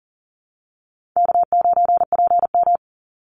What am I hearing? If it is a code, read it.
Morse code, heard as a keyed tone, about 41 wpm, 700 Hz. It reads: K9PM
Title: NA SOTA Chaser Callsigns 40wpm